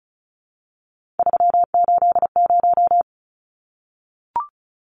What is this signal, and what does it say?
Morse code, heard as a keyed tone, about 34 wpm, 700 Hz. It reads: 380
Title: Numbers - 3-Digit 35wpm